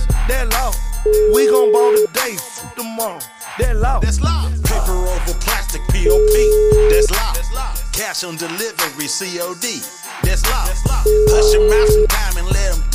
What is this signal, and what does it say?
Ringback tone (Europe, 425 Hz).